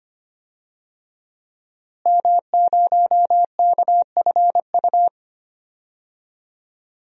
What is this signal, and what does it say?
Morse code, heard as a keyed tone, about 25 wpm, 700 Hz. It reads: M0KFU
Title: DXCC-100 Call Signs - 5 Characters - Repeated 2x - 25wpm